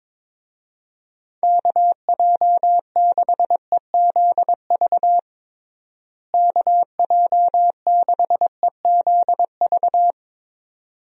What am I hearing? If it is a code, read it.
Morse code, heard as a keyed tone, about 22 wpm, 700 Hz. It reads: KJ6EZV KJ6EZV
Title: USA Call Signs - 6 Characters - Repeated 2x - 22wpm